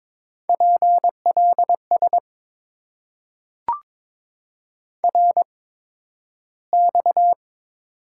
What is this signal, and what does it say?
Morse code, heard as a keyed tone, about 22 wpm, spaced as for 12 wpm, 690 Hz. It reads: PLS RX
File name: Top 90 QSO Elements - 8x Character Spacing - 22wpm